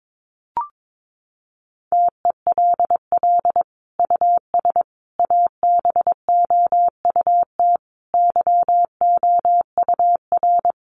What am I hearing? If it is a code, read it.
Morse code, heard as a keyed tone, about 22 wpm, 700 Hz. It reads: TELL US ABOUT YOUR
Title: Sentences from Top 1300 Words 22wpm